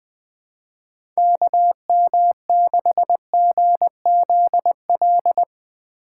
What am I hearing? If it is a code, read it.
Morse code, heard as a keyed tone, about 20 wpm, 700 Hz. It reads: KM6GZL
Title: USA Call Signs - 2x Character Spacing - 20wpm